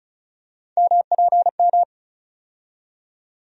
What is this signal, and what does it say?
Morse code, heard as a keyed tone, about 34 wpm, 700 Hz. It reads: MPM
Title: USA Call Sign Postfixes - 3 Letters 35wpm